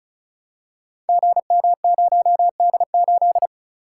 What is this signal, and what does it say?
Morse code, heard as a keyed tone, about 36 wpm, 700 Hz. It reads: GM0D8
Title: Code Groups - 5 Letter:Numbers - 35wpm